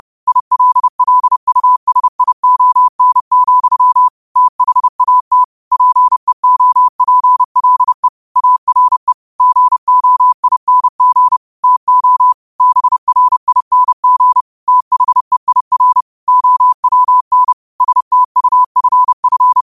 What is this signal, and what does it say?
Morse code, heard as a keyed tone, about 30 wpm, 1000 Hz. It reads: ILLUSION, THAT PEOPLE ARE GOING TO BRING THEIR OWN STUFF
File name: qotdfiles / 30wpm